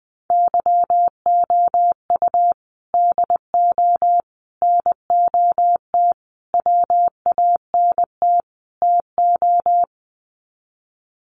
Morse code, heard as a keyed tone, about 20 wpm, 700 Hz. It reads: YOU DO NOT WANT TO
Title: Sets of 5 Words 20wpm